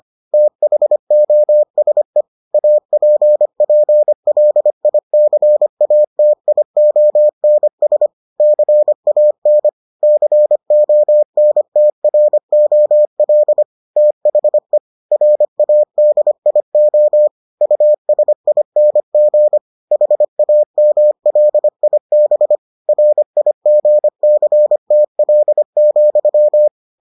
Morse code, heard as a keyed tone, about 25 wpm, 600 Hz. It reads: THOSE APPLICATIONS CAN CONTROL THE RADIO USING HAMLIB RIGCTL,